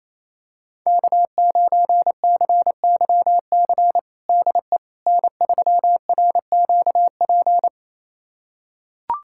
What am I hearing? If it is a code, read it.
Morse code, heard as a keyed tone, about 27 wpm, 700 Hz. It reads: K9CYC DE N3RQP
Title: USA Callsign DE Callsign - 5 Characters - 28wpm